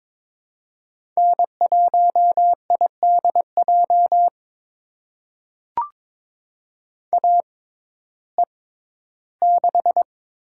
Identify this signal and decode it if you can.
Morse code, heard as a keyed tone, about 21 wpm, spaced as for 13 wpm, 700 Hz. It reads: N1IDJ AE6
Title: USA Call Signs - 5 Characters - 6x Character Spacing - 22wpm